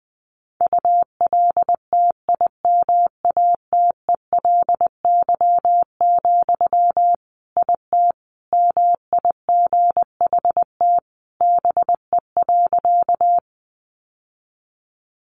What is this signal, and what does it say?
Morse code, heard as a keyed tone, about 20 wpm, 700 Hz. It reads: ULTIMATELY, IT MIGHT BE.
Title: Sentences from Top 1200 Words 20wpm